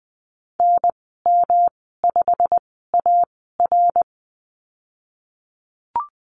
Morse code, heard as a keyed tone, about 20 wpm, spaced as for 15 wpm, 700 Hz. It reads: NM5AR
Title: 147 Lesson 24g - Letter y - Call Signs 20wpm